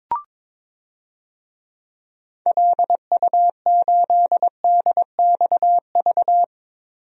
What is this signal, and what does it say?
Morse code, heard as a keyed tone, about 22 wpm, 700 Hz. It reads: LU8DXV
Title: DXCC-100 Call Signs - 6 Characters - Repeated 2x - 22wpm